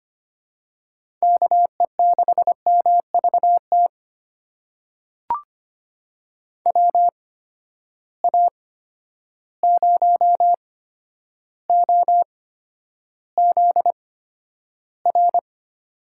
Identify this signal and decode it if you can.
Morse code, heard as a keyed tone, about 25 wpm, spaced as for 11 wpm, 700 Hz. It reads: KE6MVT WA0OZR
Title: USA Call Signs - 6 Characters - 8x Character Spacing - 25wpm